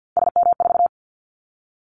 Morse code, heard as a keyed tone, about 50 wpm, 700 Hz. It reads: SM4
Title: Callsign Prefixes for Top 25 DXCC - 50wpm